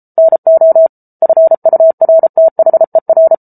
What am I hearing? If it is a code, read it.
Morse code, heard as a keyed tone, about 34 wpm, 650 Hz. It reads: NO FURTHER